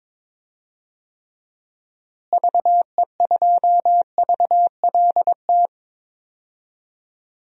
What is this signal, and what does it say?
Morse code, heard as a keyed tone, about 22 wpm, 700 Hz. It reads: VE2VLT
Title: DXCC-100 Call Signs - 6 Characters - Repeated 2x - 22wpm